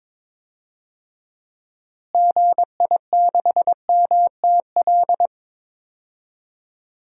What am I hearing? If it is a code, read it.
Morse code, heard as a keyed tone, about 22 wpm, 700 Hz. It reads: GI6MTL
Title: DXCC-100 Call Signs - 6 Characters - Repeated 3x - 22wpm